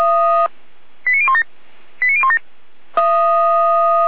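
Five-tone selective call (ZVEI), heard as quick runs of five tones, tones of about 70 ms. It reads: 89017 89017
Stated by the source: ZVEI five-tone selective call received on SDR, address 88102.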